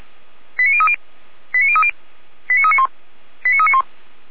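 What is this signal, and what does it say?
Five-tone selective call (ZVEI), heard as quick runs of five tones, tones of about 70 ms. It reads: 89030 89030 89391 89391
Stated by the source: ZVEI five-tone call 89504 received on SDR.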